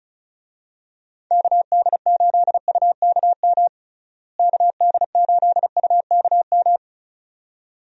Morse code, heard as a keyed tone, about 34 wpm, 700 Hz. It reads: KD8UKM KD8UKM
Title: USA Call Signs - 6 Characters - Repeated 2x - 35wpm